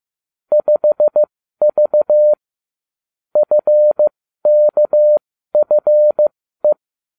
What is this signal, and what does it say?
Morse code, heard as a keyed tone, about 15 wpm, spaced as for 12 wpm, 600 Hz. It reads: 5V FKFE